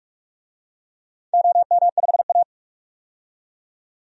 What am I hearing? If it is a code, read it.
Morse code, heard as a keyed tone, about 46 wpm, 700 Hz. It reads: OM5A
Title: DXCC-100 Call Signs - 4 Characters - 45wpm